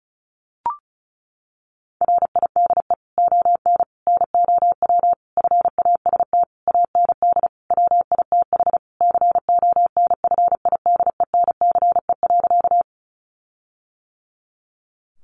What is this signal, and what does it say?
Morse code, heard as a keyed tone, about 34 wpm, 700 Hz. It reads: RIDE ON NOW FAST AND WITH CONFIDENCE.
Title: Sentences from Top 1500 Words 35wpm